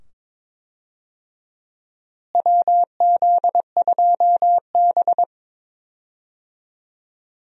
Morse code, heard as a keyed tone, about 22 wpm, 700 Hz. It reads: WZ2B
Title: USA Call Signs - 4 Characters - Repeated 2x - 22wpm